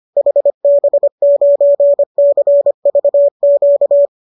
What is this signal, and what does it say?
Morse code, heard as a keyed tone, about 25 wpm, 570 Hz. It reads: HB9CVQ
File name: qso210 25